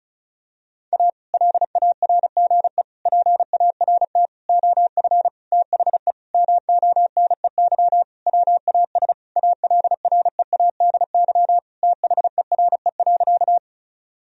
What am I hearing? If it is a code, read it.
Morse code, heard as a keyed tone, about 36 wpm, 700 Hz. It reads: A LARGE PART OF THE MONEY WAS ALREADY THERE.
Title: Sentences from Top 1000 Words - Review Part 1 - 35wpm